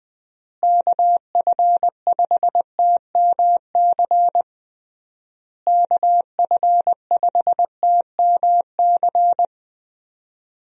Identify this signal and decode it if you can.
Morse code, heard as a keyed tone, about 19 wpm, 700 Hz. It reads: KF5TMC KF5TMC
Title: USA Call Signs - 6 Characters - Repeated 3x - 20wpm